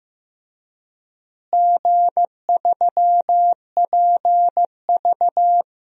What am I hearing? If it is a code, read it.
Morse code, heard as a keyed tone, about 15 wpm, 700 Hz. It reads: G3PV
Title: DXCC-100 Call Signs - 4 Characters - Repeated 3x - 15wpm